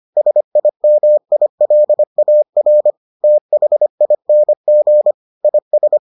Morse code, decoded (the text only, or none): SIMILAR THING IS